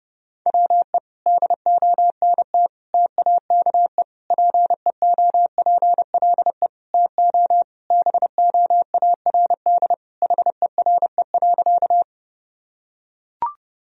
Morse code, heard as a keyed tone, about 30 wpm, 700 Hz. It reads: WE DONT TAKE PEOPLE TO BOARD HERE.
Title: Sentences from Top 1000 Words - Review Part 4 - 30wpm